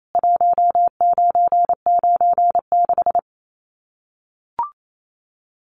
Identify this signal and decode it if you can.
Morse code, heard as a keyed tone, about 28 wpm, 700 Hz. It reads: 1996
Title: Common QSO Phrases 28wpm